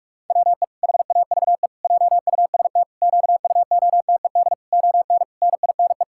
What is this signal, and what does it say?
Morse code, heard as a keyed tone, about 41 wpm, 700 Hz. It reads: WE HAVE JUST QUOTED ON NINE
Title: Pangram Challenge v2 - 45wpm